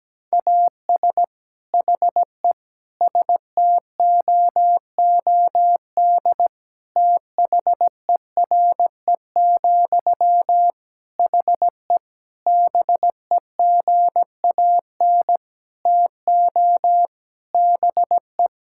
Morse code, heard as a keyed tone, about 17 wpm, 700 Hz. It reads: AS HE STOOD THERE, HE BEGAN TO BE
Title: Sentences from Top 1600 Words 17wpm